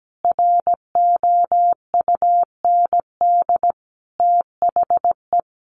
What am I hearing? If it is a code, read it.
Morse code, heard as a keyed tone, about 17 wpm, 700 Hz. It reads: ROUND THE
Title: Sentences from Top 600 Words 17wpm